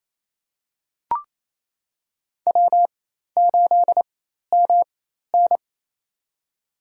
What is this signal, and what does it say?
Morse code, heard as a keyed tone, about 28 wpm, spaced as for 13 wpm, 700 Hz. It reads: W8MN
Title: USA Call Signs - 4 Characters - 4x Character Spacing - 28wpm